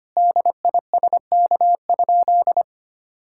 Morse code, heard as a keyed tone, about 25 wpm, 700 Hz. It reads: DISK?
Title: Sentences from Top 1000 Words - Review Part 6 - 25wpm